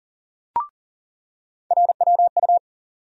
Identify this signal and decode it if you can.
Morse code, heard as a keyed tone, about 40 wpm, 700 Hz. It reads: RWU
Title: USA Call Sign Postfixes - 3 Letters 40wpm